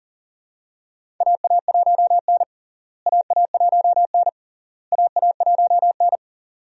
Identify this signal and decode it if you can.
Morse code, heard as a keyed tone, about 40 wpm, 700 Hz. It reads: AA1N AA1N AA1N
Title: USA Call Signs - 4 Characters - Repeated 3x - 40wpm